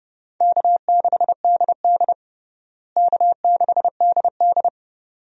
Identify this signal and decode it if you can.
Morse code, heard as a keyed tone, about 31 wpm, 700 Hz. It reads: K6DD K6DD
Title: USA Call Signs - 4 Characters - Repeated 2x - 30wpm